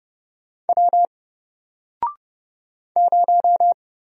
Morse code, heard as a keyed tone, about 30 wpm, 700 Hz. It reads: W 0
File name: Single-Letter-Number-30wpm